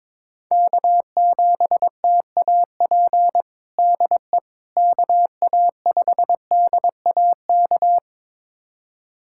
Morse code, heard as a keyed tone, about 22 wpm, 700 Hz. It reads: K7TAP DE KA5DAK
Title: USA Callsign DE Callsign 22wpm